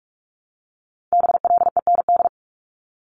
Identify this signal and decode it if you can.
Morse code, heard as a keyed tone, about 46 wpm, 700 Hz. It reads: BLEND
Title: Multi-letter Phonemes 45wpm